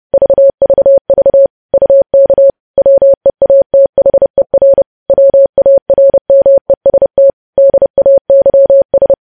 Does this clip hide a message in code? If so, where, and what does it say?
Yes: Morse code from 0.1 to 9.1 s, heard as a keyed tone, about 30 wpm, 550 Hz. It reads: VVV UK WEATHER WARMEST DAYS